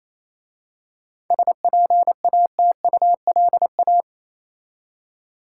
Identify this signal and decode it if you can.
Morse code, heard as a keyed tone, about 27 wpm, 700 Hz. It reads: SPATULA
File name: Easy Words - Speed-Racing with Hints - 22wpm